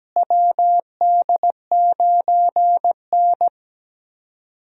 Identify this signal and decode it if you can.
Morse code, heard as a keyed tone, about 17 wpm, 700 Hz. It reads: WD9N
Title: USA Call Signs - 4 Characters - Repeated 3x - 17wpm